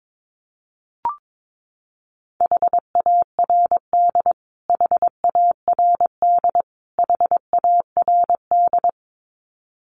Morse code, heard as a keyed tone, about 22 wpm, 700 Hz. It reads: HARD HARD HARD
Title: Top 500 Words Repeated 22wpm